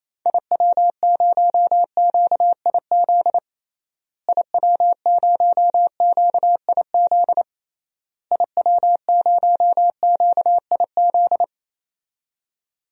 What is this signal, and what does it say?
Morse code, heard as a keyed tone, about 28 wpm, 700 Hz. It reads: IW0QIZ IW0QIZ IW0QIZ
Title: DXCC-100 Call Signs - 6 Characters - Repeated 3x - 28wpm